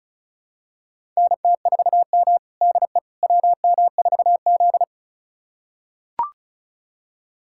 Morse code, heard as a keyed tone, about 34 wpm, 700 Hz. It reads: NT4M DE WM4Z
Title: USA Callsign DE Callsign - 4 Characters - 35wpm